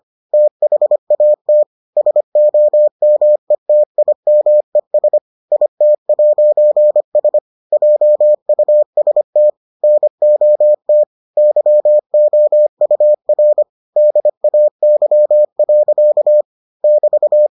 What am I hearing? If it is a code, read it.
Morse code, heard as a keyed tone, about 25 wpm, 600 Hz. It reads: THAT SOMETIMES IT'S JUST NOT YOUR DAY. <BT>